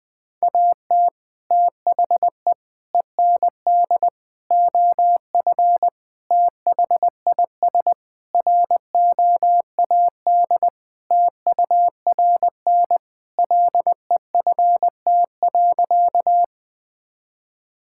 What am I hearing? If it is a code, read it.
Morse code, heard as a keyed tone, about 21 wpm, 700 Hz. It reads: AT THE END OF THIS ROAD TURN LEFT.
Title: Sentences with 5-Word Phrases - 20wpm